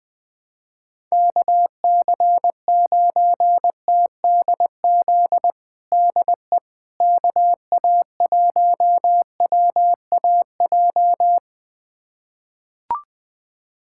Morse code, heard as a keyed tone, about 21 wpm, 700 Hz. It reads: KC9TDZ DE KA1WAJ
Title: USA Callsign DE Callsign - 6 Characters - 20wpm